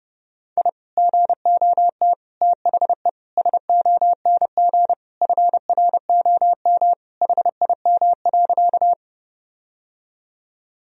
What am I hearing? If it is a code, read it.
Morse code, heard as a keyed tone, about 30 wpm, 700 Hz. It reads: I GOT THE SONG FROM HIM.
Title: Sentences from Top 1000 Words - Review Part 2 - 30wpm